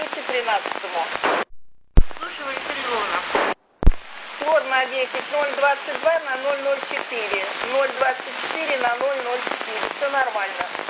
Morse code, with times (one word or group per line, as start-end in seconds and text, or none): none